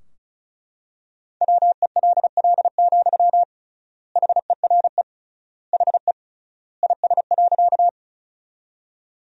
Morse code, heard as a keyed tone, about 34 wpm, 700 Hz. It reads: WELL, HERE HE IS.
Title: 3x Word-Spacing - Sentences from Top 200 Words - 35wpm